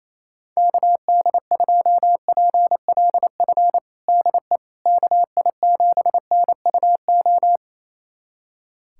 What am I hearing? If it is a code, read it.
Morse code, heard as a keyed tone, about 27 wpm, 700 Hz. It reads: KD2PLF DE KI7NUO
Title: USA Callsign DE Callsign - 6 Characters - 28wpm